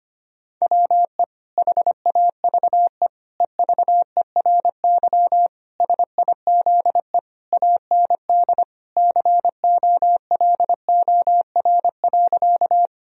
Morse code, heard as a keyed tone, about 25 wpm, 700 Hz. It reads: WE HAVE EVERY SIZE AND COLOR.
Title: Sentences from Top 1000 Words - Review Part 2 - 25wpm